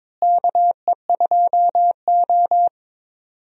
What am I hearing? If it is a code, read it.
Morse code, heard as a keyed tone, about 22 wpm, 700 Hz. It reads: KE2O
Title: USA Call Signs - 4 Characters - Repeated 2x - 22wpm